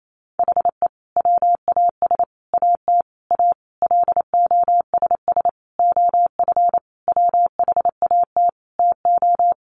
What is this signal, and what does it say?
Morse code, heard as a keyed tone, about 28 wpm, 700 Hz. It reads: HE WAS AT A LOSS OF WHAT TO
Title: Sentences from Top 1400 Words 28wpm